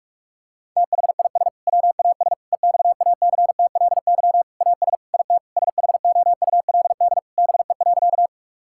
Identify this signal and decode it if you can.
Morse code, heard as a keyed tone, about 47 wpm, 700 Hz. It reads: THIS WAS EXACTLY AS IT SHOULD BE.
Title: Sentences from Top 1000 Words - Review Part 6 - 45wpm